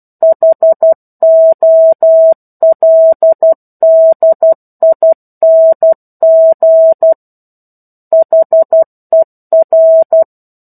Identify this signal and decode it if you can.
Morse code, heard as a keyed tone, about 12 wpm, 650 Hz. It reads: HOLDING HER
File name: 01-Chapter-01 12 WPM